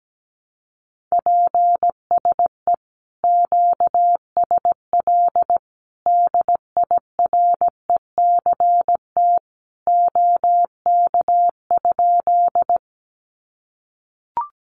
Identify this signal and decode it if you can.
Morse code, heard as a keyed tone, about 17 wpm, 700 Hz. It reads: PSE QSL DIRECT OK?
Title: Common QSO Phrases 17wpm